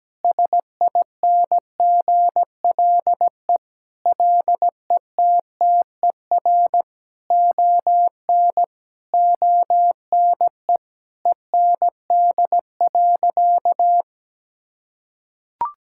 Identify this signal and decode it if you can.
Morse code, heard as a keyed tone, about 17 wpm, 700 Hz. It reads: SINGLE LETTER ON ONE END.
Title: Sentences from Top 1600 Words 17wpm